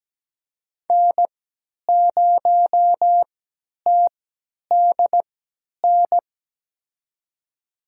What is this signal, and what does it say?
Morse code, heard as a keyed tone, about 17 wpm, spaced as for 10 wpm, 700 Hz. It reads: N0TDN
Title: USA Call Signs - 5 Characters - 3x Character Spacing - 17wpm